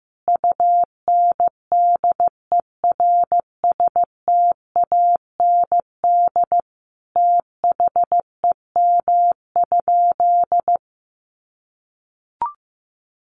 Morse code, heard as a keyed tone, about 14 wpm, 700 Hz. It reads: UNDERSTAND THEM?
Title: Sentences from Top 600 Words 15wpm